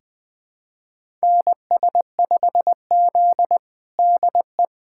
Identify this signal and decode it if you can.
Morse code, heard as a keyed tone, about 20 wpm, 700 Hz. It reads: NS5Z DE
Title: USA Callsign DE Callsign - 4 Characters - 20wpm